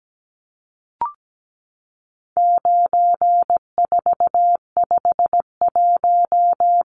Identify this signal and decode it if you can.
Morse code, heard as a keyed tone, about 17 wpm, 700 Hz. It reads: 9451
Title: Numbers - 4-Digit 17wpm